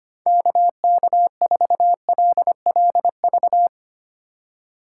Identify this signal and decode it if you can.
Morse code, heard as a keyed tone, about 25 wpm, 700 Hz. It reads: KK4LLV
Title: USA Call Signs - 6 Characters - 4x Character Spacing - 25wpm